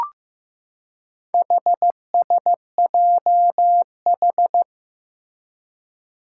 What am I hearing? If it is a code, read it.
Morse code, heard as a keyed tone, about 15 wpm, 700 Hz. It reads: HSJH
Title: Four Challenging Letter-Number Combinations 15wpm